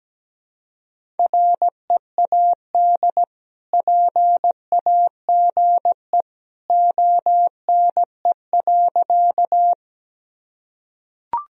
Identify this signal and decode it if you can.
Morse code, heard as a keyed tone, about 17 wpm, 700 Hz. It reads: READ PAGE ONE.
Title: Sentences from Top 1000 Words - Review Part 1 - 17wpm